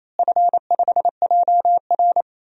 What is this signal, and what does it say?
Morse code, heard as a keyed tone, about 28 wpm, 700 Hz. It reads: F5JR
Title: DXCC-100 Call Signs - 4 Characters - Repeated 3x - 28wpm